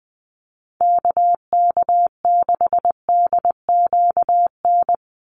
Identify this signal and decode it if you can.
Morse code, heard as a keyed tone, about 19 wpm, 700 Hz. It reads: KK6DQN
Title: USA Call Signs 20wpm